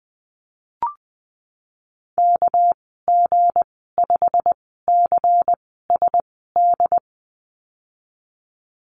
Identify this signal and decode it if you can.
Morse code, heard as a keyed tone, about 19 wpm, spaced as for 14 wpm, 700 Hz. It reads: KG5CSD
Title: 231 Lesson 37g - Letter x - Call Signs 20wpm